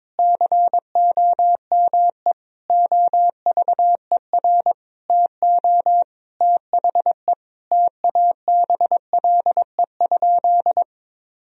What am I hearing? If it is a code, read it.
Morse code, heard as a keyed tone, about 22 wpm, 700 Hz. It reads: COME OVER TO THE TABLE?
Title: Sentences from Top 1000 Words - Review Part 2 - 22wpm